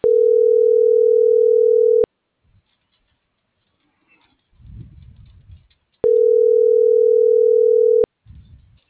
Ringback tone (North America), 440 + 480 Hz.